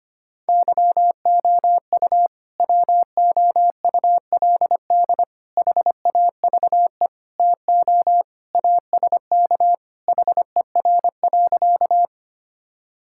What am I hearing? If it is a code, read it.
Morse code, heard as a keyed tone, about 25 wpm, 700 Hz. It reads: YOU WOULD HAVE TO ASK HER.
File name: Sentences with 3-Word Phrases - 25wpm